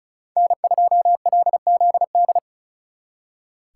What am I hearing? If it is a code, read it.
Morse code, heard as a keyed tone, about 46 wpm, 700 Hz. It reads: N2LZD